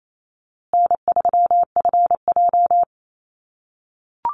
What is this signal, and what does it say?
Morse code, heard as a keyed tone, about 28 wpm, 700 Hz. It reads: N3FJ
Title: USA Call Signs Repeated 3x 28wpm